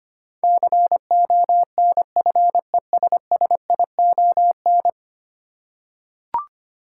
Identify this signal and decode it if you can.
Morse code, heard as a keyed tone, about 25 wpm, 700 Hz. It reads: CONFESSION
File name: 10 Letter Words - First 3 Letters Given - 25wpm